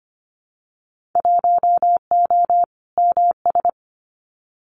Morse code, heard as a keyed tone, about 25 wpm, 700 Hz. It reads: 1O MS
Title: Winter Field Day Exchanges 25wpm